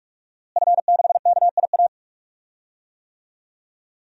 Morse code, heard as a keyed tone, about 46 wpm, 700 Hz. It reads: F6KIA
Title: DXCC-100 Call Signs - 5 Characters - Repeated 3x - 45wpm